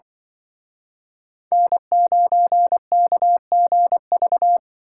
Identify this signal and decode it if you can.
Morse code, heard as a keyed tone, about 23 wpm, 700 Hz. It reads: N9KGV
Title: Speed Racing - USA 1x3 Call Signs 20wpm